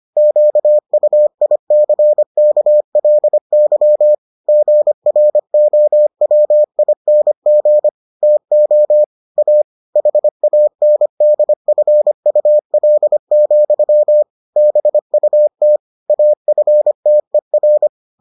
Morse code, heard as a keyed tone, about 25 wpm, 600 Hz. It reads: QUICKLY GROWING TO A HANDFUL, BUT AFTER